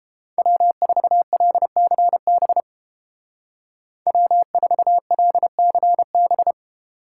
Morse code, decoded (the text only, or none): W4LCB W4LCB